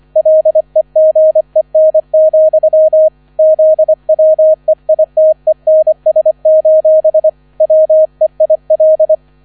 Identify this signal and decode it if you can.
Morse code, heard as a keyed tone, about 25 wpm, 630 Hz. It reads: LEGEN, ZWEITENS: WEIL